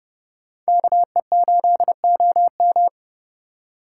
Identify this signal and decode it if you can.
Morse code, heard as a keyed tone, about 31 wpm, 700 Hz. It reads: KE8OM